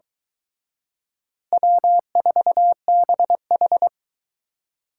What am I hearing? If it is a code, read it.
Morse code, heard as a keyed tone, about 23 wpm, 700 Hz. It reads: W4BH